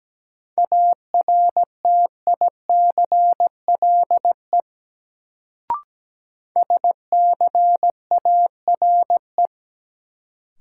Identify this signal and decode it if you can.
Morse code, heard as a keyed tone, about 17 wpm, 700 Hz. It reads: ARTICLE SCARE
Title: Sentences from Top 1000 Words - Review Part 5 - New Word Drill - 17wpm